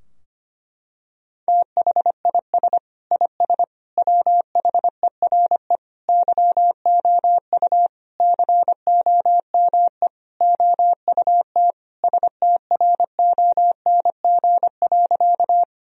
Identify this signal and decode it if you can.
Morse code, heard as a keyed tone, about 25 wpm, 700 Hz. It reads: THIS IS WHERE YOU COME OUT STRONG.
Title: Sentences from Top 1000 Words - Review Part 3 - 25wpm